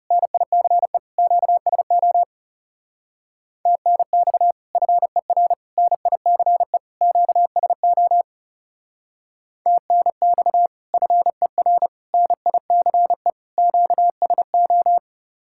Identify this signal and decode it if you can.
Morse code, heard as a keyed tone, about 39 wpm, 700 Hz. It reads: NICE QSO TNX FER NICE QSO TNX FER NICE QSO